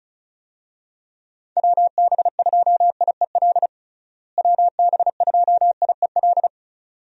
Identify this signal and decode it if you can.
Morse code, heard as a keyed tone, about 34 wpm, 700 Hz. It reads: WB2IEL WB2IEL
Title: USA Call Signs - 6 Characters - Repeated 2x - 35wpm